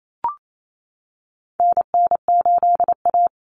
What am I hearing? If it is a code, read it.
Morse code, heard as a keyed tone, about 28 wpm, 700 Hz. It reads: NN8A
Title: Common Cut Numbers - 4-Digit 28wpm